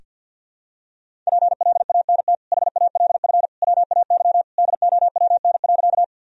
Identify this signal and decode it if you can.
Morse code, heard as a keyed tone, about 51 wpm, 700 Hz. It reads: PLANT HALF WAY DOWN.
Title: Sentences from Top 1000 Words - Review Part 1 - 50wpm